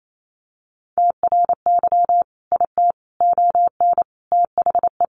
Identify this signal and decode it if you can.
Morse code, heard as a keyed tone, about 28 wpm, 700 Hz. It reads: TRY IT ON THE
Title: Sentences from Top 400 Words 28wpm